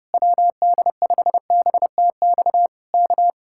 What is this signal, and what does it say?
Morse code, heard as a keyed tone, about 28 wpm, 700 Hz. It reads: WD5BTX K
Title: Speed-Racing Common QSO Phrases Encore - 30wpm